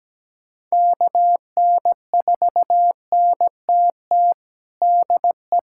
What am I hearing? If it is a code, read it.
Morse code, heard as a keyed tone, about 17 wpm, 700 Hz. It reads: KN4NTT DE
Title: USA Callsign DE Callsign 17wpm